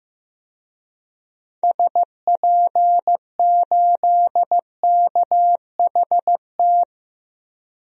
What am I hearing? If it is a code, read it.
Morse code, heard as a keyed tone, about 15 wpm, 700 Hz. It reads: SP8KHT
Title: DXCC-100 Call Signs - 6 Characters - Repeated 2x - 15wpm